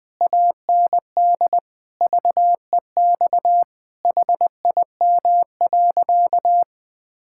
Morse code, heard as a keyed tone, about 20 wpm, 700 Hz. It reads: AND VEX HIM.
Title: Pangram Challenge v2 - 20wpm